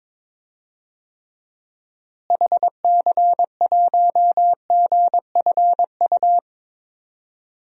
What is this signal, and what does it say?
Morse code, heard as a keyed tone, about 22 wpm, 700 Hz. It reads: HC1GFU
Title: DXCC-100 Call Signs - 6 Characters - Repeated 3x - 22wpm